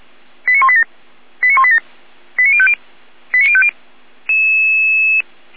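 Five-tone selective call (ZVEI), heard as quick runs of five tones, tones of about 70 ms. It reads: 89187 89187 89050 89050
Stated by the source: Five-tone ZVEI selective call 89207 received by SDR.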